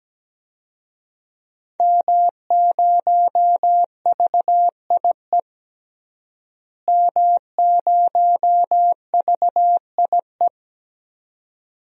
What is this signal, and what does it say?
Morse code, heard as a keyed tone, about 17 wpm, 700 Hz. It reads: M0VIE M0VIE
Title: DXCC-100 Call Signs - 5 Characters - Repeated 2x - 17wpm